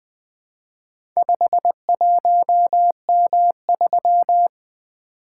Morse code, heard as a keyed tone, about 20 wpm, 700 Hz. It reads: 51M3
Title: Four Letter-Number Combinations - 20wpm